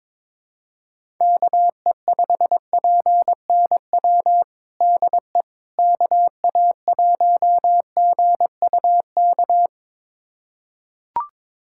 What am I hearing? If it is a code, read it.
Morse code, heard as a keyed tone, about 22 wpm, 700 Hz. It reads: KE5PNW DE KA1GUK
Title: USA Callsign DE Callsign - 6 Characters - 22wpm